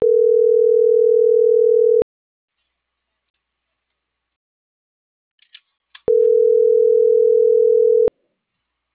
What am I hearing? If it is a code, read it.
Ringback tone (North America), 440 + 480 Hz.